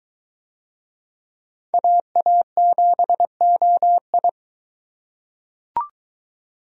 Morse code, heard as a keyed tone, about 21 wpm, 700 Hz. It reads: AA7OI